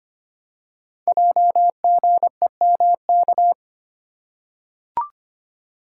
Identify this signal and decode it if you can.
Morse code, heard as a keyed tone, about 25 wpm, 700 Hz. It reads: JGEMK
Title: Code Groups - 5 Letters - 25wpm